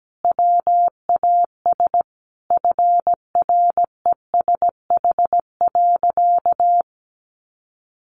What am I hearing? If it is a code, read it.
Morse code, heard as a keyed tone, about 17 wpm, 700 Hz. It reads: WAS FRESH.
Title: Sentences from Top 1200 Words 17wpm